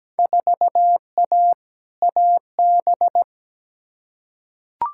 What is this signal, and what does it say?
Morse code, heard as a keyed tone, about 17 wpm, 700 Hz. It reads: 4A AB
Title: Field Day Exchanges Encore - 17wpm